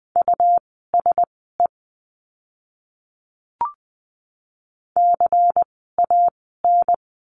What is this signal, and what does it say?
Morse code, heard as a keyed tone, about 20 wpm, spaced as for 14 wpm, 700 Hz. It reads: USE CAN
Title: 101 Lesson 17f - Letter u - Words 20wpm